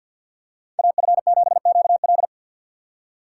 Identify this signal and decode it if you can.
Morse code, heard as a keyed tone, about 46 wpm, 700 Hz. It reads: AF7QL
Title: USA Call Signs - 5 Characters - 2x Character Spacing - 50wpm